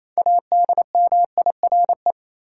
Morse code, heard as a keyed tone, about 28 wpm, 700 Hz. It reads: ADMIRE
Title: Six Letter Words - 28wpm